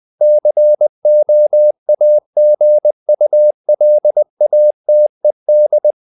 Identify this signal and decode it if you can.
Morse code, heard as a keyed tone, about 20 wpm, 600 Hz. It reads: COAGULATED
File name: the old man and the sea / the old man and the sea part 1 20wpm